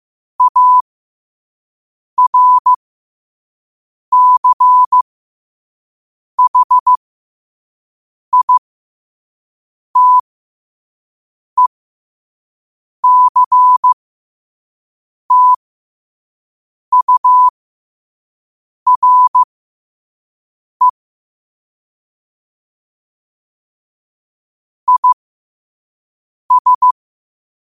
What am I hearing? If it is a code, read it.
Morse code, heard as a keyed tone, about 15 wpm, spaced as for 5 wpm, 1000 Hz. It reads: ARCHITECTURE IS